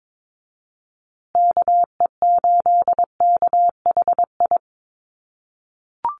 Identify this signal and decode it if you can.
Morse code, heard as a keyed tone, about 22 wpm, 700 Hz. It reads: KE8KHI
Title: USA Call Signs Repeated 2x 22wpm